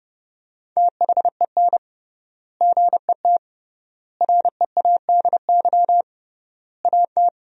Morse code, heard as a keyed tone, about 30 wpm, 700 Hz. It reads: THEN GET READY AT
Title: 3x Word-Spacing - Sentences from Top 400 Words - 30wpm